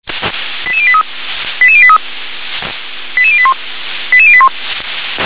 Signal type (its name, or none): five-tone selective call